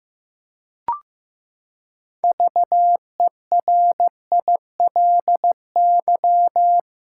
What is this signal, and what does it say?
Morse code, heard as a keyed tone, about 15 wpm, 700 Hz. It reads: VERILY
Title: Pangram Challenge v2 - 15wpm